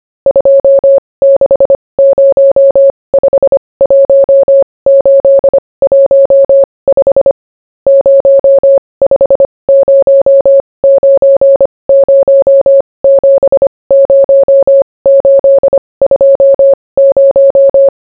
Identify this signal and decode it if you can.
Morse code, heard as a keyed tone, about 25 wpm, spaced as for 20 wpm, 550 Hz. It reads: 26051815 0509070820